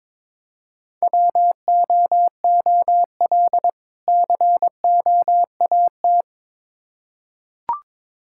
Morse code, heard as a keyed tone, about 21 wpm, 700 Hz. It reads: WOOL COAT
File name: Difficult Words with Hints - 22wpm